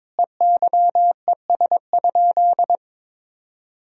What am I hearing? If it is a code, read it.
Morse code, heard as a keyed tone, about 22 wpm, 700 Hz. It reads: EYES?
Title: Sentences from Top 1000 Words - Review Part 4 - 22wpm